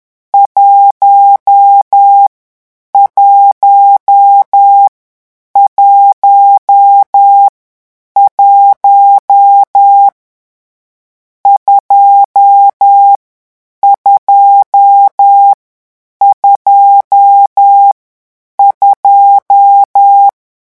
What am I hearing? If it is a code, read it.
Morse code, heard as a keyed tone, about 10 wpm, 800 Hz. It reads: 1 1 1 1 2 2 2 2
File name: morse mp3 / lecon10-vitesse 10